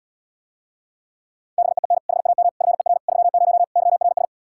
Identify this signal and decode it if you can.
Morse code, heard as a keyed tone, about 80 wpm, 700 Hz. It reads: BEEN HAM FER 30 YRS